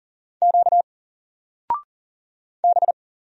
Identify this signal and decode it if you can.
Morse code, heard as a keyed tone, about 39 wpm, 700 Hz. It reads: Q B
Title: Single Character (Letter:Number:Punctuation:Prosign) Alt 40wpm